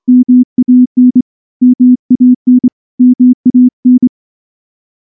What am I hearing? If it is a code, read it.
Morse code, heard as a keyed tone, about 24 wpm, 250 Hz. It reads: MAN MAN MAN
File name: morse / morse-training-25WPM